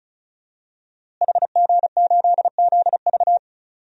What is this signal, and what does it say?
Morse code, heard as a keyed tone, about 34 wpm, 690 Hz. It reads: HG8ZV
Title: DXCC-100 Call Signs - 5 Characters - Repeated 3x - 35wpm